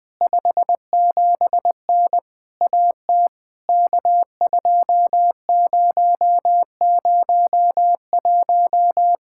Morse code, heard as a keyed tone, about 20 wpm, 700 Hz. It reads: 57N AT K2001
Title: POTA QSO Practice - 20wpm